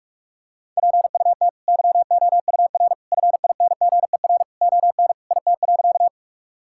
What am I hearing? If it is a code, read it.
Morse code, heard as a keyed tone, about 46 wpm, 700 Hz. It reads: PUT YOUR FINGER ON IT.
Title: Sentences from Top 1600 Words 45wpm